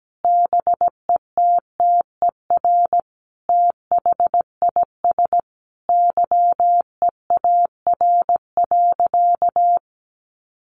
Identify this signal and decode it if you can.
Morse code, heard as a keyed tone, about 17 wpm, 700 Hz. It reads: BETTER THIS YEAR.
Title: Sentences from Top 1500 Words 17wpm